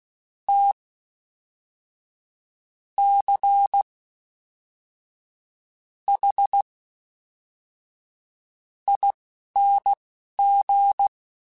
Morse code, heard as a keyed tone, about 16 wpm, spaced as for 5 wpm, 800 Hz. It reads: TCHING
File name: Special INGs / Special INGs-mp3s / Fast ing @10